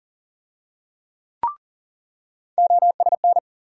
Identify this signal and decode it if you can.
Morse code, heard as a keyed tone, about 40 wpm, 700 Hz. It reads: OSN
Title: USA Call Sign Postfixes - 3 Letters 40wpm